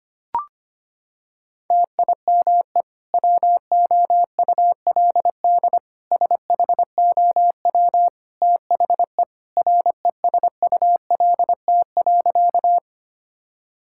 Morse code, heard as a keyed tone, about 25 wpm, 700 Hz. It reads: TIME WOULD SHOW THE RESULT.
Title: Sentences from Top 1000 Words - Review Part 5 - 25wpm